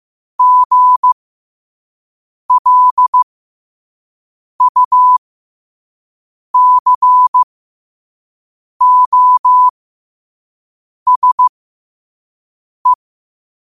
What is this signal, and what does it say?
Morse code, heard as a keyed tone, about 15 wpm, spaced as for 5 wpm, 1000 Hz. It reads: GLUCOSE
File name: 5wpm